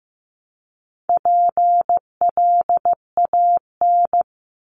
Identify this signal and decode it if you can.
Morse code, heard as a keyed tone, about 15 wpm, 700 Hz. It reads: PLAN
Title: ING Suffix - Top 100 Words with Base Words 15wpm